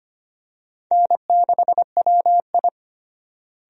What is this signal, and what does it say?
Morse code, heard as a keyed tone, about 25 wpm, 700 Hz. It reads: N6WI
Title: USA Call Signs - 4 Characters - Repeated 3x - 25wpm